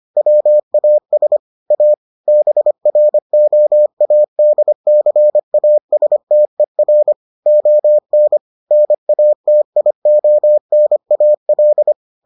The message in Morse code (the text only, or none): WAS A BROADCASTER ON NATIONAL